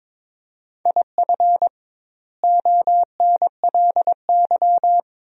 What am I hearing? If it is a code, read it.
Morse code, heard as a keyed tone, about 22 wpm, 700 Hz. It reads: IF ONLY
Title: 2x Word-Spacing - Sentences from Top 500 Words - 22wpm